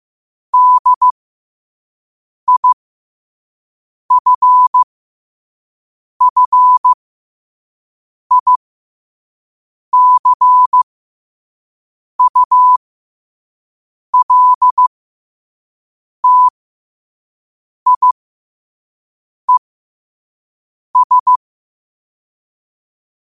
Morse code, heard as a keyed tone, about 15 wpm, spaced as for 5 wpm, 1000 Hz. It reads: DIFFICULTIES
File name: qotdfiles / 5wpm